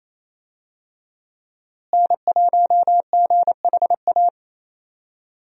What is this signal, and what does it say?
Morse code, heard as a keyed tone, about 27 wpm, 700 Hz. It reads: N1GHA
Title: USA Call Signs - 5 Characters - Repeated 2x - 28wpm